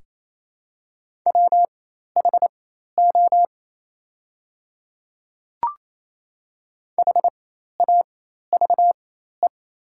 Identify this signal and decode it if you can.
Morse code, heard as a keyed tone, about 28 wpm, spaced as for 13 wpm, 700 Hz. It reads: WHO HAVE
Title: 4x Character Spacing - Top 100 Words - 28wpm